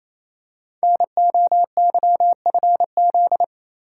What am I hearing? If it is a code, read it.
Morse code, heard as a keyed tone, about 28 wpm, 700 Hz. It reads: NOYFZ
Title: Code Groups - 5 Letters - 28wpm